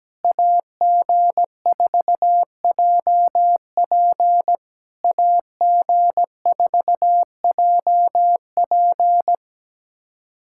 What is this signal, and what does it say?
Morse code, heard as a keyed tone, about 17 wpm, 690 Hz. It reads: AG4JP AG4JP
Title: SOTA QSO Practice 17wpm